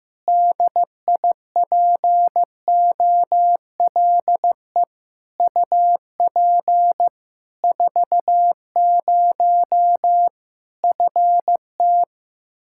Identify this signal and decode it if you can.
Morse code, heard as a keyed tone, about 15 wpm, 700 Hz. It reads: DIPOLE UP 40 FT
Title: Practice QSO - Exchange 2 - Info - 15wpm